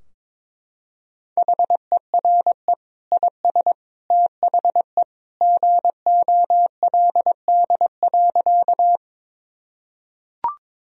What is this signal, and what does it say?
Morse code, heard as a keyed tone, about 22 wpm, 700 Hz. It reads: HERE IS THE GOLD.
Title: Sentences from Top 1000 Words - Review Part 3 - 22wpm